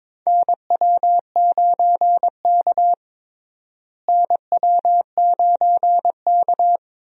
Morse code, heard as a keyed tone, about 21 wpm, 700 Hz. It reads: NW9K NW9K
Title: USA Call Signs - 4 Characters - Repeated 2x - 22wpm